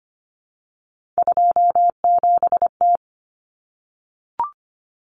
Morse code, heard as a keyed tone, about 25 wpm, 700 Hz. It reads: 27T
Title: Common Cut Numbers - 3-Digit 25wpm